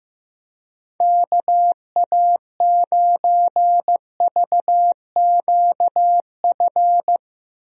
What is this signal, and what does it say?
Morse code, heard as a keyed tone, about 15 wpm, 700 Hz. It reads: KA9VQF
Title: USA Call Signs - 6 Characters - Repeated 2x - 15wpm